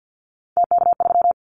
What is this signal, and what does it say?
Morse code, heard as a keyed tone, about 50 wpm, 700 Hz. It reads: TK3
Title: Callsign Prefixes for Top 100 DXCC - 50wpm